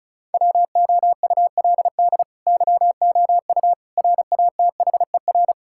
Morse code, heard as a keyed tone, about 36 wpm, 700 Hz. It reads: WOULD YOU RATHER
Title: Sentences from Top 1000 Words - Review Part 6 - 35wpm